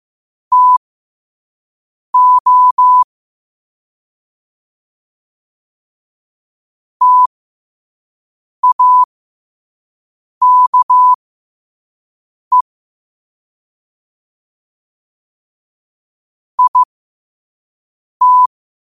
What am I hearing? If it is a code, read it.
Morse code, heard as a keyed tone, about 15 wpm, spaced as for 5 wpm, 1000 Hz. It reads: TO TAKE IT
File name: qotdfiles / 5wpm